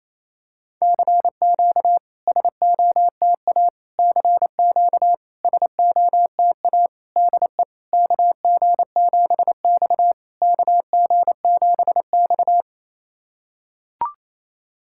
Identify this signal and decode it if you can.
Morse code, heard as a keyed tone, about 28 wpm, 700 Hz. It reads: CQ SOTA CQ SOTA DE KG7X KG7X
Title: SOTA QSO Practice 28wpm